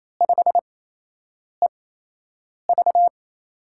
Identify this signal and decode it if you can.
Morse code, heard as a keyed tone, about 28 wpm, spaced as for 8 wpm, 700 Hz. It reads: 5EV
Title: USA Call Signs - 4 Characters - 8x Character Spacing - 28wpm